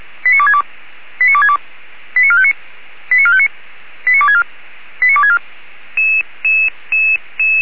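Five-tone selective call (ZVEI), heard as quick runs of five tones, tones of about 70 ms. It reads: 89262 89262 89459 89459 89264 89264
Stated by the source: ZVEI five-tone selective call received with an SDR.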